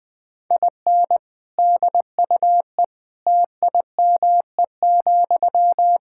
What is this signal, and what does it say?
Morse code, heard as a keyed tone, about 21 wpm, 700 Hz. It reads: IN DUE TIME,
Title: Sentences from Top 1000 Words - Review Part 6 - 20wpm